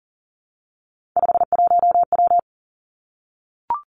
Morse code, heard as a keyed tone, about 37 wpm, 700 Hz. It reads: 51W
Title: Three Challenging Letter-Number Combinations 40wpm